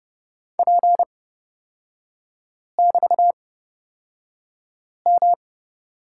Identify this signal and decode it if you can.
Morse code, heard as a keyed tone, about 30 wpm, spaced as for 5 wpm, 700 Hz. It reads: P<BT>M
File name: Lesson 36e - BT - Three Characters 30wpm